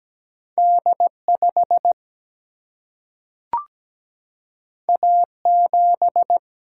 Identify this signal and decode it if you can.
Morse code, heard as a keyed tone, about 17 wpm, 700 Hz. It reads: D5 A7
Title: Single-Letter Single-Number 17wpm